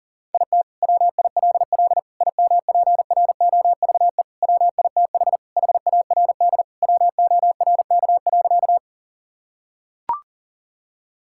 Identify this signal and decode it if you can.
Morse code, heard as a keyed tone, about 40 wpm, 700 Hz. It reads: IT WILL IMPROVE WITH HARD WORK.
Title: Sentences from Top 1000 Words - Review Part 5 - 40wpm